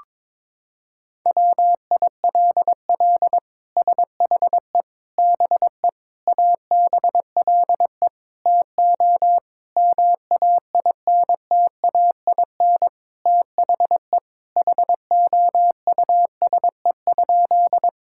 Morse code, heard as a keyed tone, about 22 wpm, 700 Hz. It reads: WILL SHE BE ABLE TO MAINTAIN THE HOUSE?
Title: Sentences from Top 1000 Words - Review Part 5 - 22wpm